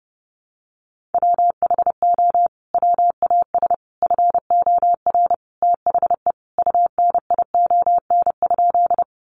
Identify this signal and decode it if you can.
Morse code, heard as a keyed tone, about 30 wpm, 700 Hz. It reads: WHO WAS FOR THE UNION?
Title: Sentences from Top 1400 Words 30wpm